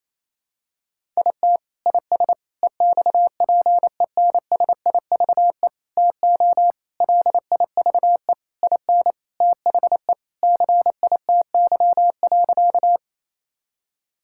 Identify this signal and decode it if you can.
Morse code, heard as a keyed tone, about 28 wpm, 700 Hz. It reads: IT IS EXPENSIVE TO LIVE IN THE CITY.
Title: Sentences from Top 1000 Words - Review Part 6 - 28wpm